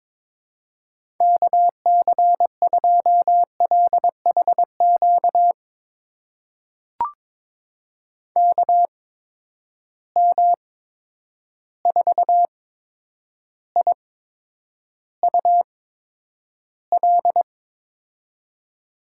Morse code, heard as a keyed tone, about 22 wpm, spaced as for 9 wpm, 700 Hz. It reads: KC2LHQ KM4IUL
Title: USA Call Signs - 6 Characters - 8x Character Spacing - 22wpm